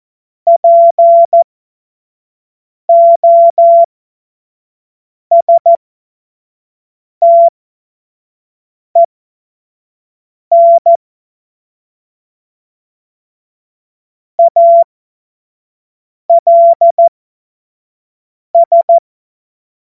Morse code, heard as a keyed tone, about 14 wpm, spaced as for 5 wpm, 680 Hz. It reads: POSTEN ALS
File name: Morsekurs / Witze / Witz 18. Juli 2023 25 F70